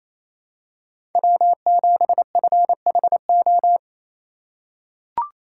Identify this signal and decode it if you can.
Morse code, heard as a keyed tone, about 28 wpm, 700 Hz. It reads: W7FHO
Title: USA Call Signs - 4x Character Spacing - 28wpm